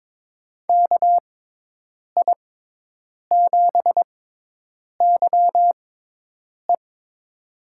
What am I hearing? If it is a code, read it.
Morse code, heard as a keyed tone, about 22 wpm, spaced as for 8 wpm, 700 Hz. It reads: KI7YE
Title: USA Call Signs - 6 Characters - 6x Character Spacing - 22wpm